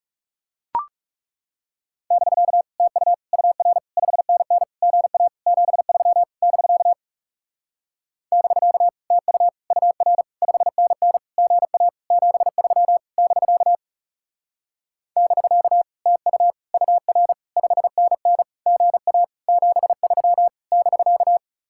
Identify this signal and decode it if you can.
Morse code, heard as a keyed tone, about 37 wpm, 700 Hz. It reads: <BK> TU UR 5NN GA 73 <BK> <BK> TU UR 5NN GA 73 <BK> <BK> TU UR 5NN GA 73 <BK>
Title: Speed-Racing Common QSO Phrases - 30wpm